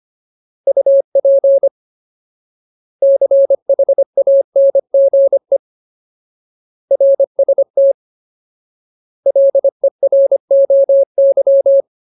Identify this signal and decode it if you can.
Morse code, heard as a keyed tone, about 25 wpm, 550 Hz. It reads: UP CHANGE RST LEROY
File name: Ultimate Word List / 25WPM Ultimate